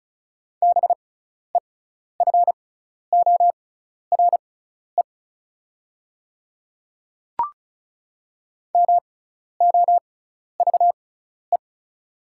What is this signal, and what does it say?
Morse code, heard as a keyed tone, about 36 wpm, spaced as for 12 wpm, 690 Hz. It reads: BEFORE MOVE
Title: 6x Character Spacing - Top 200 Words - 35wpm